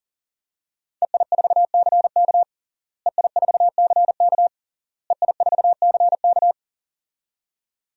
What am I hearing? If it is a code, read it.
Morse code, heard as a keyed tone, about 40 wpm, 700 Hz. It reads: EI4CK EI4CK EI4CK
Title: DXCC-100 Call Signs - 5 Characters - Repeated 3x - 40wpm